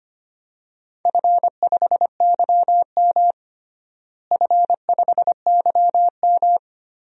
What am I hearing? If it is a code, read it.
Morse code, heard as a keyed tone, about 25 wpm, 700 Hz. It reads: F5YM F5YM
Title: DXCC-100 Call Signs - 4 Characters - Repeated 2x - 25wpm